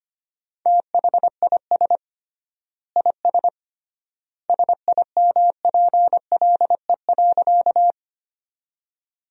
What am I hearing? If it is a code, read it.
Morse code, heard as a keyed tone, about 25 wpm, 700 Hz. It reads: THIS IS SIMPLE.
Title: 3x Word-Spacing - Sentences from Top 400 Words - 25wpm